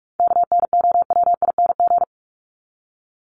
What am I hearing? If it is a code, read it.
Morse code, heard as a keyed tone, about 44 wpm, 700 Hz. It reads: KNOWING
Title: ING Suffix - Top 100 Words with Base Words 45wpm